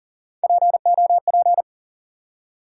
Morse code, heard as a keyed tone, about 40 wpm, 700 Hz. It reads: POP
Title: Three Letter Words - 40wpm